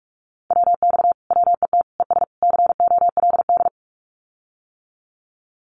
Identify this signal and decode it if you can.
Morse code, heard as a keyed tone, about 46 wpm, 700 Hz. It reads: WX WET ES COLD
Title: Common QSO Phrases Encore 45wpm